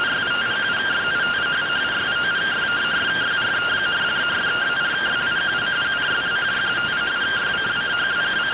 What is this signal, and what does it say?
Two-tone teleprinter signal, about 170 Hz shift.